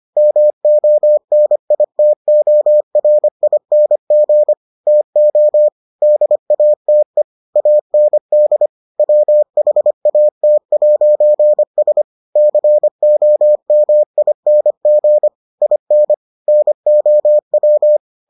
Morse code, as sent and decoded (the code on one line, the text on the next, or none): -- --- -. .. - --- .-. .. -. --. / - --- / -.. .- - . / .- -. -.. / .-- .... .- - .----. ... / -.-. --- -- .. -. --. / .. -. / -. --- .--
MONITORING TO DATE AND WHAT'S COMING IN NOW